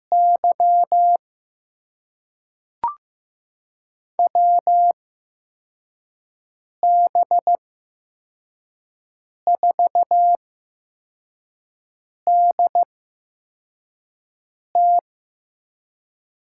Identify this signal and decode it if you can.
Morse code, heard as a keyed tone, about 15 wpm, spaced as for 4 wpm, 700 Hz. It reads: Y WB4DT
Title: USA Call Signs - 5 Characters - 8x Character Spacing - 15wpm